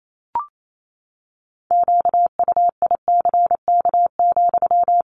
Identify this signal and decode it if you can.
Morse code, heard as a keyed tone, about 28 wpm, 700 Hz. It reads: QUICK,
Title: Sentences from Top 500 Words 28wpm